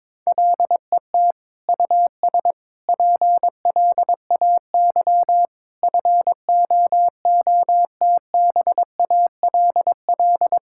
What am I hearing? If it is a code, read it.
Morse code, heard as a keyed tone, about 21 wpm, 700 Hz. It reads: LET US PLAY FOOTBALL
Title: Sentences from Top 1600 Words 22wpm